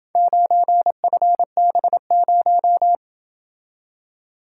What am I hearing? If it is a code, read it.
Morse code, heard as a keyed tone, about 31 wpm, 700 Hz. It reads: 9FB0